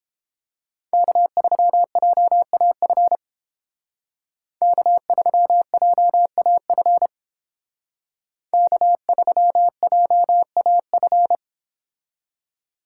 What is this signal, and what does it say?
Morse code, heard as a keyed tone, about 30 wpm, 700 Hz. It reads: K3JAF K3JAF K3JAF